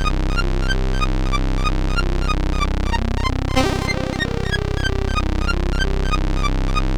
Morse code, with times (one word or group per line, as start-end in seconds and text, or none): none